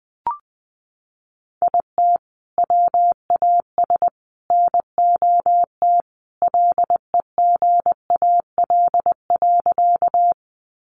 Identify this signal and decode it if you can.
Morse code, heard as a keyed tone, about 20 wpm, 700 Hz. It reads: IT WAS NOT LEGAL.
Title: Sentences from Top 1200 Words 20wpm